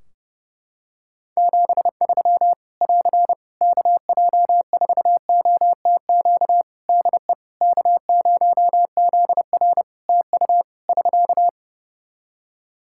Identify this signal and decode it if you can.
Morse code, heard as a keyed tone, about 30 wpm, 700 Hz. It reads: 73 <AR> KJ4OTQ DE K0ZR TU <SK>
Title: Practice QSO - Exchange 4 - Ending - 30wpm